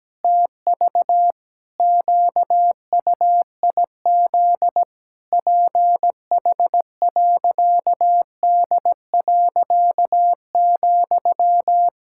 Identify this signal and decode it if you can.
Morse code, heard as a keyed tone, about 17 wpm, 700 Hz. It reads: TV QUIZ PH.D.,
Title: Pangram Challenge v2 - 17wpm